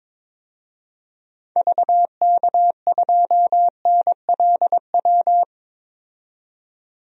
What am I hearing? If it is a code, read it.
Morse code, heard as a keyed tone, about 22 wpm, 700 Hz. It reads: VK2NLW
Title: DXCC-100 Call Signs - 6 Characters - 22wpm